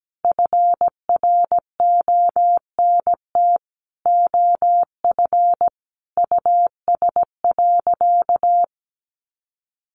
Morse code, heard as a keyed tone, about 17 wpm, 700 Hz. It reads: FRONT OF US.
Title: Sentences from Top 500 Words 17wpm